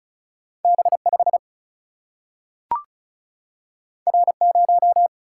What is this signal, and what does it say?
Morse code, heard as a keyed tone, about 36 wpm, 700 Hz. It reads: B 5 R 0
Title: Single-Letter Single-Number 35wpm